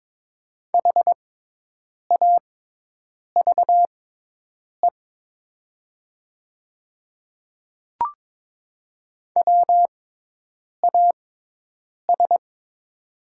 Morse code, heard as a keyed tone, about 22 wpm, spaced as for 8 wpm, 700 Hz. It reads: HAVE WAS
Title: 6x Character Spacing - Top 200 Words - 22wpm